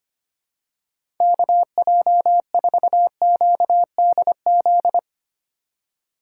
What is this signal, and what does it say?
Morse code, heard as a keyed tone, about 25 wpm, 700 Hz. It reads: KJ4QDZ
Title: USA Call Signs - 4x Character Spacing - 25wpm